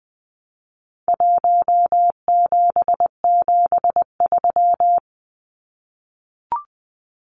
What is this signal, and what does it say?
Morse code, heard as a keyed tone, about 20 wpm, 700 Hz. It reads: 1773
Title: Numbers - Contest Style - 4-Digit 20wpm